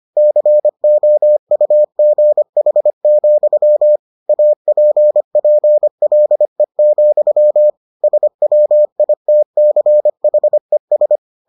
Morse code, heard as a keyed tone, about 25 wpm, 600 Hz. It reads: COUGH, APPLE, SWITCHES